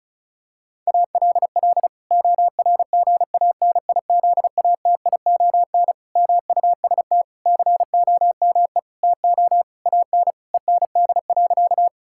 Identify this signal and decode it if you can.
Morse code, heard as a keyed tone, about 36 wpm, 700 Hz. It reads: ALL ORGANIZATION MUST COME TO AN END.
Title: Sentences from Top 1000 Words - Review Part 5 - 35wpm